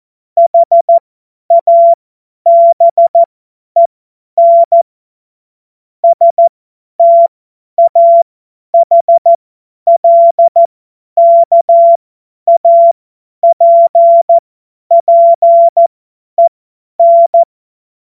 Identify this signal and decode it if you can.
Morse code, heard as a keyed tone, about 14 wpm, spaced as for 10 wpm, 680 Hz. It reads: HABEN STAHLKAPPEN
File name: Morsekurs / Witze / Witz 05 Dezember 2023 50 F70